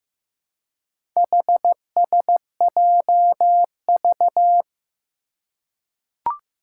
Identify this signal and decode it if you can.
Morse code, heard as a keyed tone, about 15 wpm, 700 Hz. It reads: HSJV
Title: Four Challenging Letter-Number Combinations 15wpm